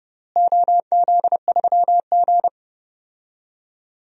Morse code, heard as a keyed tone, about 30 wpm, 700 Hz. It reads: OZ3G
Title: DXCC-100 Call Signs - 4 Characters - Repeated 2x - 30wpm